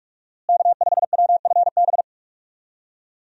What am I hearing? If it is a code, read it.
Morse code, heard as a keyed tone, about 46 wpm, 700 Hz. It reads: K5WFB
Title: USA Call Signs - 5 Characters - Repeated 2x - 45wpm